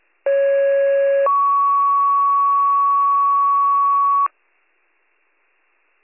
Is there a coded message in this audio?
Yes: two-tone page from 0.3 to 4.3 s, heard as a one-second tone, then a three-second tone, about 550 then 1120 Hz.